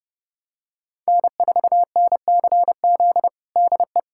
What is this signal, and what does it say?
Morse code, heard as a keyed tone, about 30 wpm, 700 Hz. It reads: N4NCZ DE
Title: USA Callsign DE Callsign - 5 Characters - 30wpm